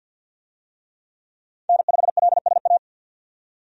Morse code, heard as a keyed tone, about 52 wpm, 700 Hz. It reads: N5LSA
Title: USA Call Signs - 5 Characters - 50wpm